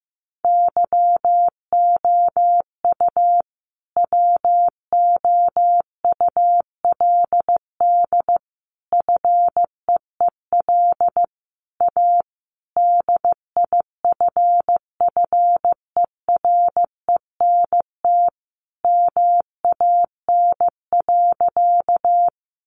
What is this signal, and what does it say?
Morse code, heard as a keyed tone, about 15 wpm, 700 Hz. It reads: YOU WOULD FEEL A DIFFERENT MAN.
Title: Sentences from Top 600 Words 15wpm